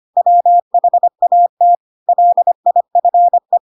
Morse code, heard as a keyed tone, about 25 wpm, 700 Hz. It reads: WHAT LIFE
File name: Pinocchio / Pinocchio ch7 25 25 ik1hge